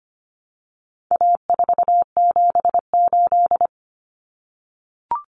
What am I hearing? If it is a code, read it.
Morse code, heard as a keyed tone, about 25 wpm, 700 Hz. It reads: A478
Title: Common Cut Numbers - 4-Digit 25wpm